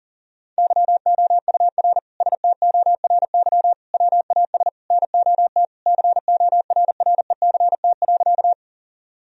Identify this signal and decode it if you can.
Morse code, heard as a keyed tone, about 40 wpm, 700 Hz. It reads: YOUR STORY WAS NOT CORRECT.
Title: Sentences from Top 1000 Words - Review Part 2 - 40wpm